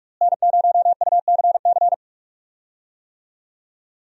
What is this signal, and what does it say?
Morse code, heard as a keyed tone, about 44 wpm, 700 Hz. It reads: N0UCC
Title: USA Call Signs - 5 Characters - Repeated 2x - 45wpm